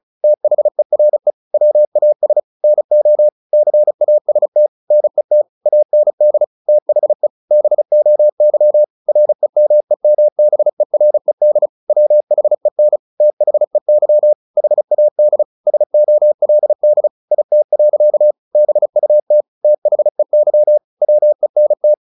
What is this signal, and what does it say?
Morse code, heard as a keyed tone, about 34 wpm, 600 Hz. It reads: THERE WAS NO CAST NET AND THE BOY REMEMBERED WHEN THEY HAD SOLD IT. BUT THEY WENT
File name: the old man and the sea / the old man and the sea 1 35wpm